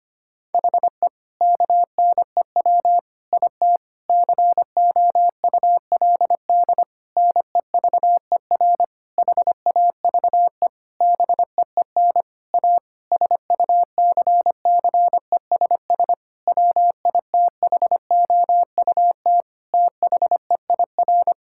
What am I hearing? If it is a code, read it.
Morse code, heard as a keyed tone, about 25 wpm, 700 Hz. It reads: HE KNEW IT COULD NEVER HAVE BEEN A SUCCESS WITHOUT THEIR
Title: Sentences from Top 1000 Words - Review Part 6 - 25wpm